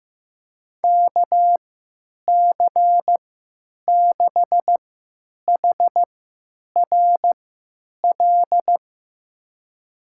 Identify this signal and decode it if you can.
Morse code, heard as a keyed tone, about 15 wpm, spaced as for 9 wpm, 700 Hz. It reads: KC6HRL
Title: USA Call Signs - 6 Characters - 3x Character Spacing - 15wpm